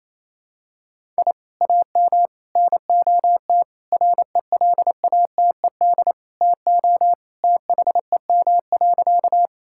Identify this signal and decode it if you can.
Morse code, heard as a keyed tone, about 27 wpm, 700 Hz. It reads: I AM NOT RELATED TO THEM.
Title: Sentences from Top 1000 Words - Review Part 5 - 28wpm